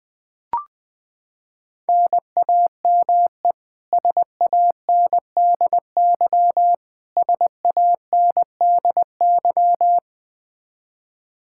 Morse code, heard as a keyed tone, about 20 wpm, 700 Hz. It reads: NAME SANDY SANDY
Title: Practice QSO - Exchange 1 - Beginning - 20wpm